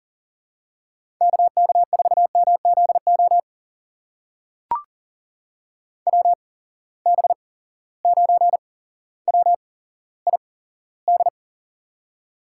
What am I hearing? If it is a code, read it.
Morse code, heard as a keyed tone, about 42 wpm, spaced as for 17 wpm, 700 Hz. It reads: KK4MZO WB9WID
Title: USA Call Signs - 6 Characters - 8x Character Spacing - 40wpm